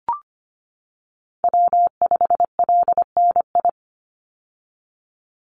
Morse code, heard as a keyed tone, about 25 wpm, 700 Hz. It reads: W5LNI
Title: Call Signs - 25wpm